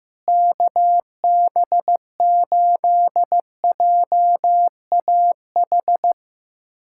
Morse code, heard as a keyed tone, about 16 wpm, 700 Hz. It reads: KB8JAH
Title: USA Callsign DE Callsign 15wpm